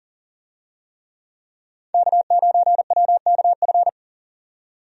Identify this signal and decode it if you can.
Morse code, heard as a keyed tone, about 38 wpm, 700 Hz. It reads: K9WKF
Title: USA Call Signs - 5 Characters - 40wpm